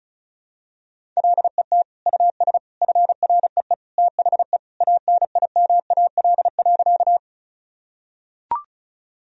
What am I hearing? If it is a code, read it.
Morse code, heard as a keyed tone, about 36 wpm, 700 Hz. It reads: LET US FREE THE ANIMAL.
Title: Sentences from Top 1000 Words - Review Part 1 - 35wpm